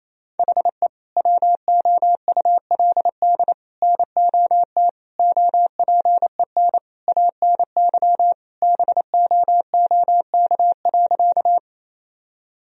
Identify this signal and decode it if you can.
Morse code, heard as a keyed tone, about 27 wpm, 700 Hz. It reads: HE WOULD NOT OPEN ANY BOOK.
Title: Sentences from Top 1000 Words - Review Part 1 - 28wpm